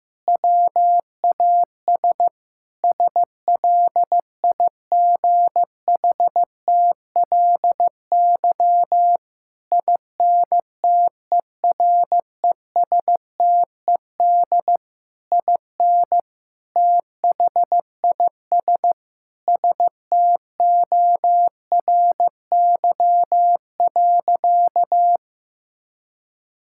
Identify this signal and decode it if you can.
Morse code, heard as a keyed tone, about 16 wpm, 700 Hz. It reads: WAS SLIGHTLY INTERESTED IN THIS STORY.
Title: Sentences from Top 1600 Words 15wpm